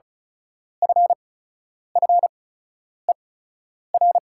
Morse code, heard as a keyed tone, about 36 wpm, spaced as for 10 wpm, 690 Hz. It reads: FFER
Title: 8x Character Spacing - Top 500 Words - 35wpm